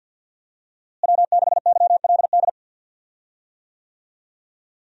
Morse code, heard as a keyed tone, about 50 wpm, 700 Hz. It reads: W6YLD
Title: USA Call Signs - 5 Characters - Repeated 3x - 50wpm